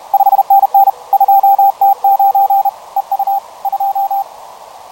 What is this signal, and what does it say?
Morse code, heard as a keyed tone, about 31 wpm, 790 Hz, with radio noise. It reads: 5NN 2T9 EU 2
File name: 20m CW